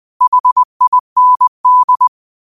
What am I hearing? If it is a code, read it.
Morse code, heard as a keyed tone, about 20 wpm, 1000 Hz. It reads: HIND
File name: qotdfiles / 20wpm